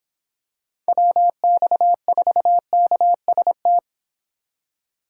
Morse code, heard as a keyed tone, about 25 wpm, 700 Hz. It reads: WX4KST
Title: Speed Racing - USA Call Signs 22wpm